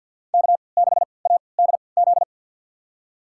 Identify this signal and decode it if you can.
Morse code, heard as a keyed tone, about 47 wpm, spaced as for 28 wpm, 700 Hz. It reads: K6ADZ
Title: USA Call Signs - 5 Characters - 3x Character Spacing - 50wpm